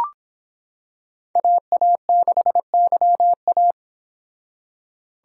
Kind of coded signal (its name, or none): Morse code